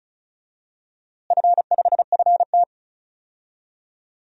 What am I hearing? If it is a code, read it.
Morse code, heard as a keyed tone, about 36 wpm, 700 Hz. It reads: F5FT
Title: DXCC-100 Call Signs - 4 Characters - Repeated 3x - 35wpm